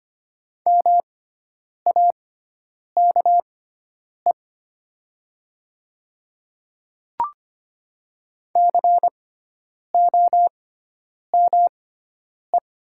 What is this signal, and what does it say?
Morse code, heard as a keyed tone, about 25 wpm, spaced as for 9 wpm, 700 Hz. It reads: MAKE COME
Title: 6x Character Spacing - Top 100 Words - 25wpm